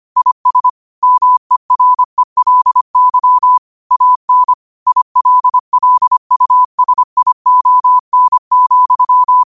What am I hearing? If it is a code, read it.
Morse code, heard as a keyed tone, about 25 wpm, 1000 Hz. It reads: IS MERELY AN ILLUSION,
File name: qotdfiles / 25wpm